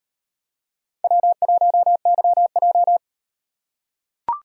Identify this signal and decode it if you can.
Morse code, heard as a keyed tone, about 38 wpm, 700 Hz. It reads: W1YJ